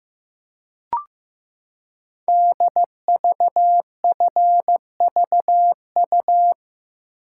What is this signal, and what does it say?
Morse code, heard as a keyed tone, about 15 wpm, 700 Hz. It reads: DVFVU
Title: Code Groups - 5 Letters - 15wpm